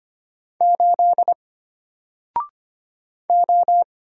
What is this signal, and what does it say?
Morse code, heard as a keyed tone, about 25 wpm, 700 Hz. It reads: 8 O
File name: Single-Letter-Number-25wpm